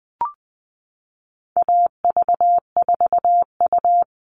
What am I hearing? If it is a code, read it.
Morse code, heard as a keyed tone, about 20 wpm, 700 Hz. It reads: AV4U
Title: All Cut Numbers - 4-Digit 20wpm